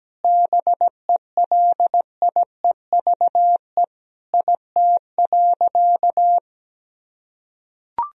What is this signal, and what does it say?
Morse code, heard as a keyed tone, about 17 wpm, 700 Hz. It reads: BELIEVE IT.
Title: Sentences from Top 1000 Words - Review Part 1 - 17wpm